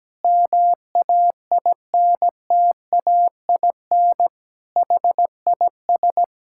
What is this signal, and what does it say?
Morse code, heard as a keyed tone, about 17 wpm, 700 Hz. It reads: MAINTAIN HIS
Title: Sentences from Top 1000 Words - Review Part 5 - 17wpm